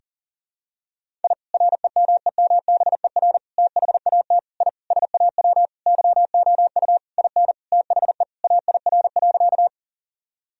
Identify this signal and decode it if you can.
Morse code, heard as a keyed tone, about 40 wpm, 700 Hz. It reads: I REMEMBER THAT I SAW YOU IN THE AIR.
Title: Sentences from Top 1000 Words - Review Part 1 - 40wpm